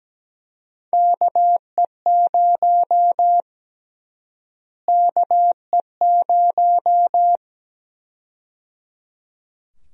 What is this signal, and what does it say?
Morse code, heard as a keyed tone, about 17 wpm, 700 Hz. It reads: KE0 KE0
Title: USA Call Sign Prefixes - Repeated 2x - 17wpm